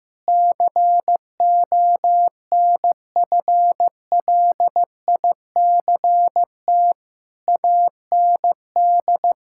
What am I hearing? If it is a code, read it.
Morse code, heard as a keyed tone, about 15 wpm, 700 Hz. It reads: CONFLICT AND
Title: Sentences from Top 1600 Words 15wpm